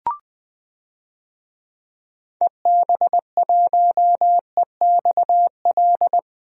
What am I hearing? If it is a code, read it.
Morse code, heard as a keyed tone, about 21 wpm, 700 Hz. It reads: EB1EXL
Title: DXCC-100 Call Signs - 6 Characters - 20wpm